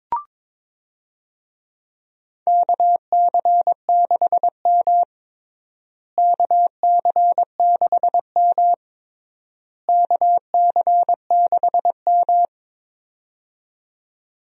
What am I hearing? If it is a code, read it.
Morse code, heard as a keyed tone, about 22 wpm, 700 Hz. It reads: KC6M KC6M KC6M
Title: USA Call Signs - 4 Characters - Repeated 3x - 22wpm